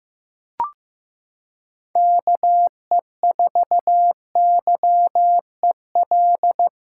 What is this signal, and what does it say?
Morse code, heard as a keyed tone, about 15 wpm, 700 Hz. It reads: KE4YEL
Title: USA Callsign DE Callsign 15wpm